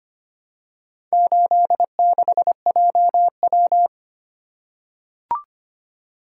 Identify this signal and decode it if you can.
Morse code, heard as a keyed tone, about 25 wpm, 700 Hz. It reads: 86JW
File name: practice / 25wpm / Four Letter-Number Combinations - 25wpm